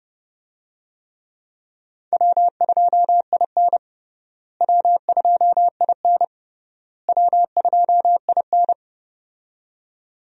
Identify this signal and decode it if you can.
Morse code, heard as a keyed tone, about 30 wpm, 700 Hz. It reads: W2IN W2IN W2IN
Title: USA Call Signs - 4 Characters - Repeated 3x - 30wpm